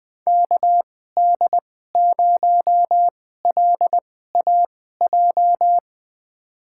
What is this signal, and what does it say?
Morse code, heard as a keyed tone, about 20 wpm, spaced as for 14 wpm, 700 Hz. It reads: KD0LAJ
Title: USA Call Signs - 6 Characters - 2x Character Spacing - 20wpm